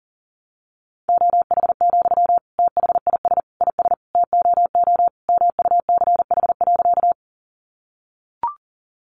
Morse code, heard as a keyed tone, about 40 wpm, 700 Hz. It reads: OH, THIS IS TOO MUCH.
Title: Sentences from Top 500 Words 40wpm